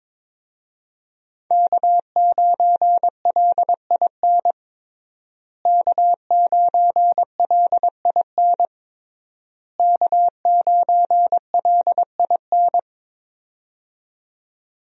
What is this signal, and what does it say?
Morse code, heard as a keyed tone, about 22 wpm, 700 Hz. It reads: K9LIN K9LIN K9LIN
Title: USA Call Signs - 5 Characters - Repeated 3x - 22wpm